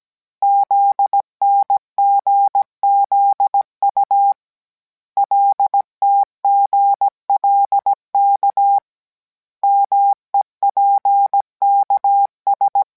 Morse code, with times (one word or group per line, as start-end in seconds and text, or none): 0.4-4.3 ZNGZU
5.2-8.8 LTGLK
9.6-12.8 MEPKS